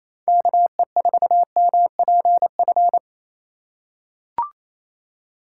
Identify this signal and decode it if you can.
Morse code, heard as a keyed tone, about 28 wpm, 700 Hz. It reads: KE4MPF
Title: USA Call Signs - 6 Characters - 3x Character Spacing - 28wpm